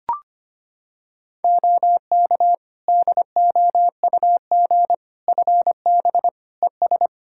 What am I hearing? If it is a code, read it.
Morse code, heard as a keyed tone, about 25 wpm, 700 Hz. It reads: OK DOUG FB ES
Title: K1USN SST QSO Practice 25wpm